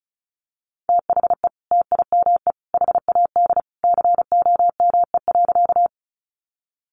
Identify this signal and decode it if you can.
Morse code, heard as a keyed tone, about 34 wpm, 700 Hz. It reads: THE TIME HAD COME.
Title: Sentences from Top 100 Words 35wpm